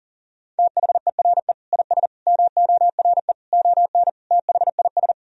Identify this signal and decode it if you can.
Morse code, heard as a keyed tone, about 40 wpm, 700 Hz. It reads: THERE IS MORE ON THIS
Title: Sentences from Top 1000 Words - Review Part 1 - 40wpm